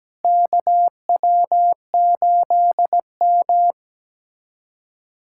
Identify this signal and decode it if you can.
Morse code, heard as a keyed tone, about 17 wpm, 700 Hz. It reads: KW8M
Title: USA Callsign DE Callsign - 4 Characters - 17wpm